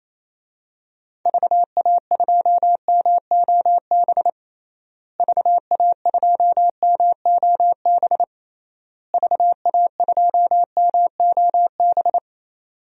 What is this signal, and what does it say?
Morse code, heard as a keyed tone, about 28 wpm, 700 Hz. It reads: VA2MOB VA2MOB VA2MOB
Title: DXCC-100 Call Signs - 6 Characters - Repeated 3x - 28wpm